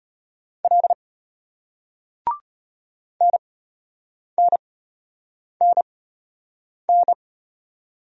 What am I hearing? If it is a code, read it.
Morse code, heard as a keyed tone, about 35 wpm, spaced as for 8 wpm, 700 Hz. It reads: L NNNN